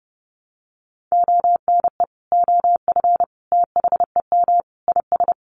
Morse code, heard as a keyed tone, about 30 wpm, 700 Hz. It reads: ONE OF THEM IS
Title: Additional Easy Phrases 30wpm